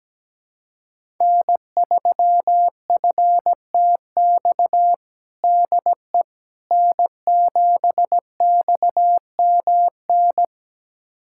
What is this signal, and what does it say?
Morse code, heard as a keyed tone, about 17 wpm, 700 Hz. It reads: N3FTX DE N7XMN
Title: USA Callsign DE Callsign - 5 Characters - 17wpm